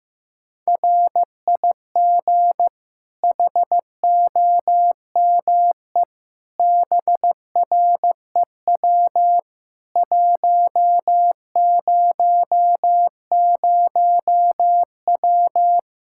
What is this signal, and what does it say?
Morse code, heard as a keyed tone, about 15 wpm, 700 Hz. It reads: RIG HOME BREW 100W
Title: Practice QSO - Exchange 2 - Info - 15wpm